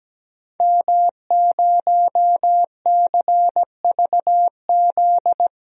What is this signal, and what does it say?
Morse code, heard as a keyed tone, about 17 wpm, 700 Hz. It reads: M0CVZ
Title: DXCC-100 Call Signs - 5 Characters - 17wpm